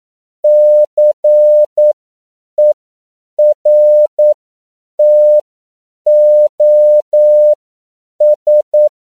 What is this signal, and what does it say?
Morse code, heard as a keyed tone, about 9 wpm, spaced as for 7 wpm, 600 Hz. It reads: CERTOS
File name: CW / 10PPM